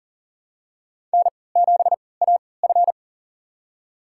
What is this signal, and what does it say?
Morse code, heard as a keyed tone, about 40 wpm, spaced as for 23 wpm, 700 Hz. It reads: N7AF
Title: USA Call Signs - 4 Characters - 3x Character Spacing - 40wpm